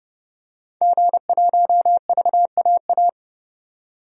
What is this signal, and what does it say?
Morse code, heard as a keyed tone, about 30 wpm, 700 Hz. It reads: G1VAA
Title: DXCC-100 Call Signs - 5 Characters - 30wpm